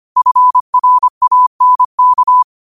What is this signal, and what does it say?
Morse code, heard as a keyed tone, about 25 wpm, 1000 Hz. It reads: FRANK